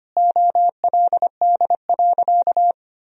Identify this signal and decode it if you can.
Morse code, heard as a keyed tone, about 25 wpm, 700 Hz. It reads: OLD.
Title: Sentences from Top 1000 Words - Review Part 2 - 25wpm